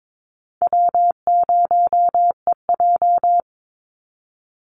Morse code, heard as a keyed tone, about 22 wpm, 700 Hz. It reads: W0EJ
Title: Easy Call Signs Repeated 2x - 22wpm